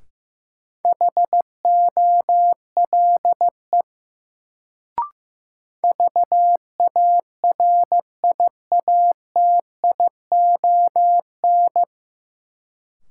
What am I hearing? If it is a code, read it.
Morse code, heard as a keyed tone, about 15 wpm, 700 Hz. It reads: HOLE VARIATION
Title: Sentences from Top 1600 Words - New Word Drill Alt - 15wpm